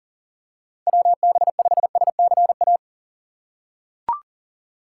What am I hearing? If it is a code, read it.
Morse code, heard as a keyed tone, about 40 wpm, 700 Hz. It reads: WB5SCA
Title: USA Call Signs - 6 Characters - 3x Character Spacing - 40wpm